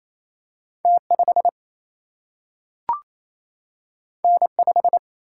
Morse code, heard as a keyed tone, about 28 wpm, 700 Hz. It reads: T 5 N 5
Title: Single-Letter Single-Number 28wpm